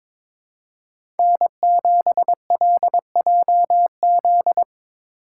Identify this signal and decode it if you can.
Morse code, heard as a keyed tone, about 22 wpm, 700 Hz. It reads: N7LJZ
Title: USA Call Signs - 5 Characters - 3x Character Spacing - 22wpm